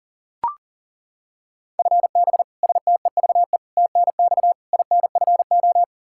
Morse code, heard as a keyed tone, about 40 wpm, 700 Hz. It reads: FB STEVE TNX INFO
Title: Practice QSO - Exchange 3 - More Info - 40wpm